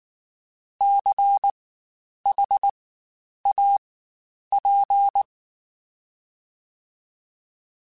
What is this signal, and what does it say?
Morse code, heard as a keyed tone, about 19 wpm, spaced as for 9 wpm, 800 Hz. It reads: CHAP
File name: Spaced Alphas / Spaced Alphas mp3s / Cs sp @18